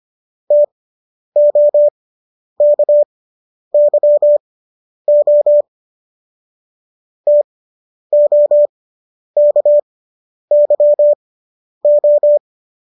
Morse code, heard as a keyed tone, about 25 wpm, spaced as for 10 wpm, 590 Hz. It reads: TOKYO TOKYO